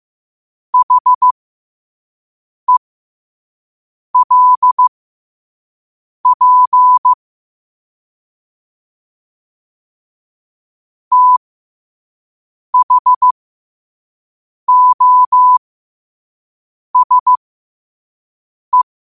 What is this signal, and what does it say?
Morse code, heard as a keyed tone, about 15 wpm, spaced as for 5 wpm, 1000 Hz. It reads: HELP THOSE
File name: qotdfiles / 5wpm